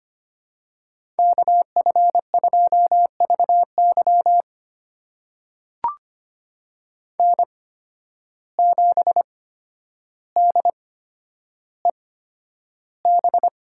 Morse code, heard as a keyed tone, about 25 wpm, spaced as for 11 wpm, 700 Hz. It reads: KF2VY N7DEB
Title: USA Call Signs - 5 Characters - 8x Character Spacing - 25wpm